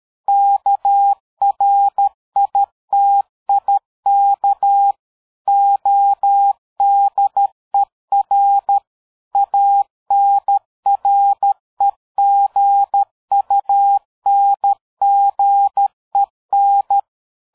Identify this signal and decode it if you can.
Morse code, heard as a keyed tone, about 13 wpm, 800 Hz. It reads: KRITIK ODER ANREGUNGEN
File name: dk5ke / PersWorte 60